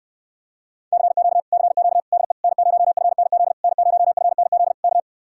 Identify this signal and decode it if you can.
Morse code, heard as a keyed tone, about 75 wpm, 700 Hz. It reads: CQ CQ DE N0FNZ N0FNZ K